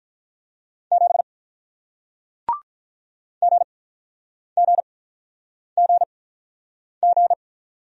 Morse code, heard as a keyed tone, about 48 wpm, spaced as for 9 wpm, 700 Hz. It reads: 7 GGGG